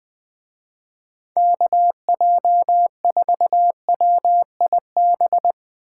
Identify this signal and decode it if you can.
Morse code, heard as a keyed tone, about 19 wpm, 700 Hz. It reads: KJ4WIB
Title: Speed Racing - USA 2x3 Call Signs 20wpm